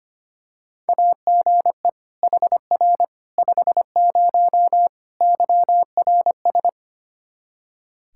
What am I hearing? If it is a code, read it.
Morse code, heard as a keyed tone, about 25 wpm, 700 Hz. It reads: AGE HR 50 YRS
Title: K1USN SST QSO Practice 25wpm